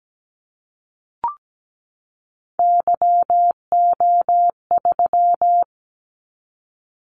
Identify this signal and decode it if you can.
Morse code, heard as a keyed tone, about 17 wpm, 700 Hz. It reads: YO3
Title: Callsign Prefixes for Top 25 DXCC - 17wpm